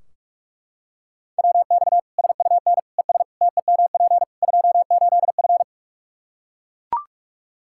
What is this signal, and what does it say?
Morse code, heard as a keyed tone, about 41 wpm, 700 Hz. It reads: WX SUN ES TEMP 28F
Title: Practice QSO - Exchange 2 - Info - 45wpm